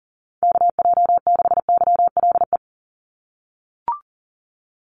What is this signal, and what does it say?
Morse code, heard as a keyed tone, about 40 wpm, 700 Hz. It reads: KJ6YLE
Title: USA Call Signs 40wpm